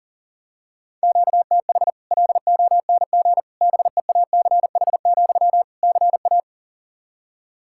Morse code, heard as a keyed tone, about 37 wpm, 700 Hz. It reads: QTH LONG BEACH, CA
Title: Practice QSO - Exchange 1 - Beginning - 40wpm